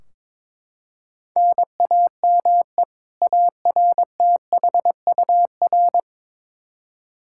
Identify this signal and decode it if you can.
Morse code, heard as a keyed tone, about 22 wpm, 700 Hz. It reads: NAME ARTHUR
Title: Practice QSO - Exchange 1 - Beginning - 22wpm